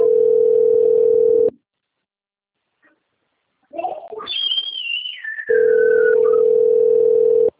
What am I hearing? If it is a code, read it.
Ringback tone (North America), 440 + 480 Hz.